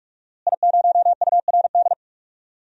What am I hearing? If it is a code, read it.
Morse code, heard as a keyed tone, about 46 wpm, 700 Hz. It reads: I0URD
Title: DXCC-100 Call Signs - 5 Characters - Repeated 3x - 45wpm